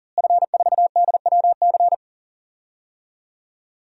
Morse code, heard as a keyed tone, about 40 wpm, 700 Hz. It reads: F4DWC
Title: DXCC-100 Call Signs - 5 Characters - Repeated 2x - 40wpm